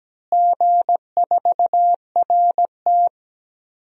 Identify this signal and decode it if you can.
Morse code, heard as a keyed tone, about 17 wpm, 700 Hz. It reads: G4RT
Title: DXCC-100 Call Signs - 4 Characters - Repeated 2x - 17wpm